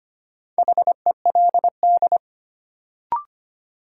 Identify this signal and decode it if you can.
Morse code, heard as a keyed tone, about 25 wpm, 700 Hz. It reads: HELD
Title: Sentences from Top 1600 Words - New Word Drill Alt - 25wpm